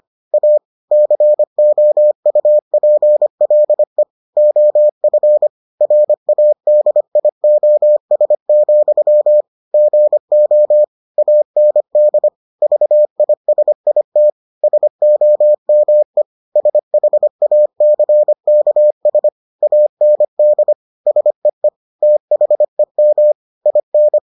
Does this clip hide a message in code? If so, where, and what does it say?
Yes: Morse code from 0.3 to 24.2 s, heard as a keyed tone, about 25 wpm, 600 Hz. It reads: A COUPLE OF RADIOS, GO AND VISIT SOME SHACKS AND SEE THEM IN